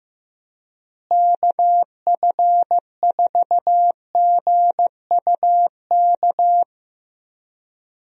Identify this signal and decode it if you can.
Morse code, heard as a keyed tone, about 15 wpm, 700 Hz. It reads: KF4GUK
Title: USA Call Signs - 6 Characters - Repeated 2x - 15wpm